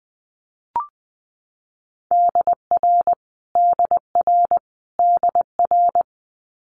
Morse code, heard as a keyed tone, about 20 wpm, 700 Hz. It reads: DR DR DR
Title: Multi-letter Phonemes Repeated 3x 20wpm